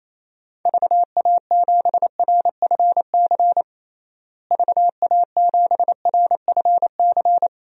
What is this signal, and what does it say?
Morse code, heard as a keyed tone, about 28 wpm, 700 Hz. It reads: VA7RFC VA7RFC
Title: DXCC-100 Call Signs - 6 Characters - Repeated 3x - 28wpm